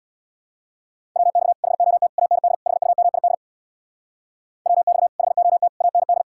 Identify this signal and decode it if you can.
Morse code, heard as a keyed tone, about 65 wpm, 700 Hz. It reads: WX HOT AND HUMID WX HOT AND